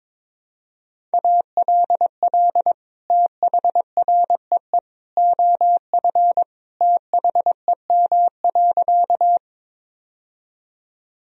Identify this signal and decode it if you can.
Morse code, heard as a keyed tone, about 22 wpm, 700 Hz. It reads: ALL THREE OF THEM.
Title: Sentences from Top 1000 Words - Review Part 1 - 22wpm